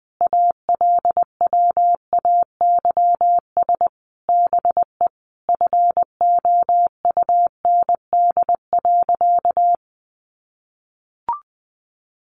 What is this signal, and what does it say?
Morse code, heard as a keyed tone, about 20 wpm, 700 Hz. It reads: ALWAYS BE FOUND.
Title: Sentences from Top 1300 Words 20wpm